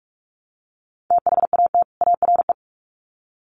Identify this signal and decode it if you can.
Morse code, heard as a keyed tone, about 46 wpm, 700 Hz. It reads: THAT ARE
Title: Sets of 2 Words 45wpm